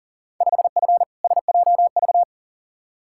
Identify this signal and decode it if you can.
Morse code, heard as a keyed tone, about 40 wpm, 700 Hz. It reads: 5F SJV
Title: Field Day Exchanges Encore - 40wpm